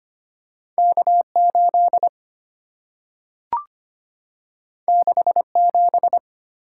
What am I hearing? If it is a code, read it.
Morse code, heard as a keyed tone, about 25 wpm, 700 Hz. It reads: K 8 6 7
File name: Two Letter-Number Combinations - 25wpm